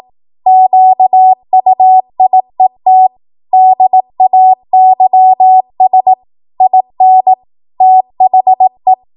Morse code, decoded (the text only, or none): QUIET DAYS IN THE